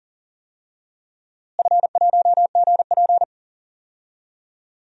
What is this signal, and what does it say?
Morse code, heard as a keyed tone, about 39 wpm, 700 Hz. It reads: F1GP
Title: DXCC-100 Call Signs - 4 Characters - Repeated 3x - 40wpm